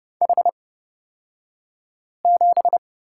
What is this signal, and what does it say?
Morse code, heard as a keyed tone, about 31 wpm, spaced as for 5 wpm, 700 Hz. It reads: H7
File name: Lesson 36d - BT - Two Characters 30wpm